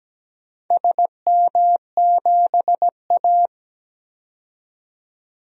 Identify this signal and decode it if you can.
Morse code, heard as a keyed tone, about 17 wpm, 700 Hz. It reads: SM7A
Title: DXCC-100 Call Signs - 4 Characters - 17wpm